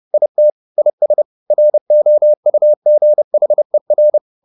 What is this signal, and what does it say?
Morse code, heard as a keyed tone, about 30 wpm, 600 Hz. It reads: IT IS ROUGHER
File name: the old man and the sea / the old man and the sea 1 30wpm